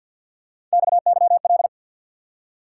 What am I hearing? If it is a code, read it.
Morse code, heard as a keyed tone, about 48 wpm, 700 Hz. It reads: XYL
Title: Top 180 QSO Elements 50wpm